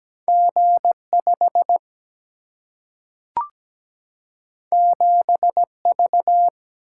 Morse code, heard as a keyed tone, about 17 wpm, 700 Hz. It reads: G5 7V
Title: Single-Letter Single-Number 17wpm